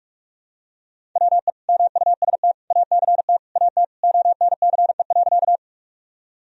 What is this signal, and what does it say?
Morse code, heard as a keyed tone, about 43 wpm, 700 Hz. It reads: WE MUST ACT AT ONCE.
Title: Sentences from Top 1000 Words - Review Part 1 - 45wpm